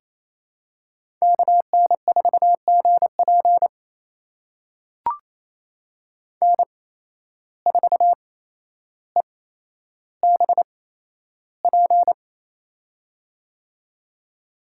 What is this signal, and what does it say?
Morse code, heard as a keyed tone, about 28 wpm, spaced as for 12 wpm, 700 Hz. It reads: KN4GP N4EBP
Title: USA Call Signs - 5 Characters - 8x Character Spacing - 28wpm